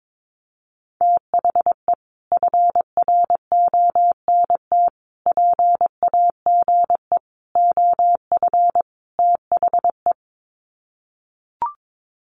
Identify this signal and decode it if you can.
Morse code, heard as a keyed tone, about 22 wpm, 700 Hz. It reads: THE FRONT PAGE OF THE
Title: Sets of 5 Words Encore 22wpm